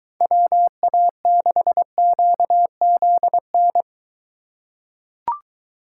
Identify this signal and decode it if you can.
Morse code, heard as a keyed tone, about 23 wpm, 700 Hz. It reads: WA6QZN